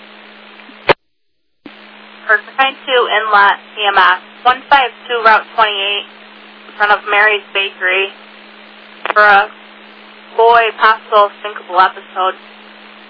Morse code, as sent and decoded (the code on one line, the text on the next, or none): none